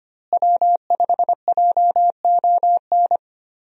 Morse code, heard as a keyed tone, about 25 wpm, 700 Hz. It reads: W5JON
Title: USA Call Signs - 5 Characters - Repeated 2x - 25wpm